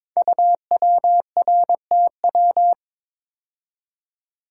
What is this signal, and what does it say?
Morse code, heard as a keyed tone, about 22 wpm, 700 Hz. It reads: UWRTW
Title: Code Groups - 5 Letters - 22wpm